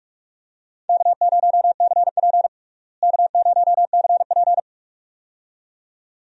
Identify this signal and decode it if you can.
Morse code, heard as a keyed tone, about 47 wpm, 700 Hz. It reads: K0CP K0CP
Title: USA Call Signs - 4 Characters - Repeated 2x - 45wpm